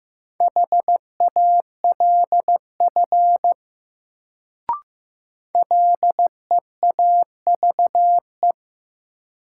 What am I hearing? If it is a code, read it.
Morse code, heard as a keyed tone, about 15 wpm, 700 Hz. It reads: HALF LEAVE
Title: Sentences from Top 400 Words - New Word Drill - 15wpm